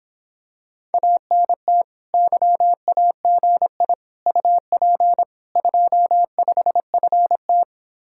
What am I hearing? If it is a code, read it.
Morse code, heard as a keyed tone, about 26 wpm, 700 Hz. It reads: ANT YAGI UP 25FT